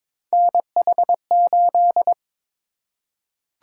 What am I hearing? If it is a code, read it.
Morse code, heard as a keyed tone, about 22 wpm, 700 Hz. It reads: NH8
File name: practice / 22wpm / Three Letter-Number Combinations - 22wpm